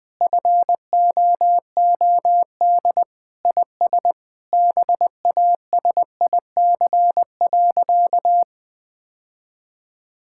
Morse code, heard as a keyed tone, about 21 wpm, 700 Hz. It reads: FOOD IS BASIC.
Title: Sentences with 3-Word Phrases - 20wpm